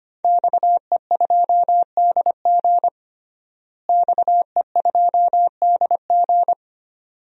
Morse code, heard as a keyed tone, about 25 wpm, 700 Hz. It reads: XE2DG XE2DG
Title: DXCC-100 Call Signs - 5 Characters - Repeated 2x - 25wpm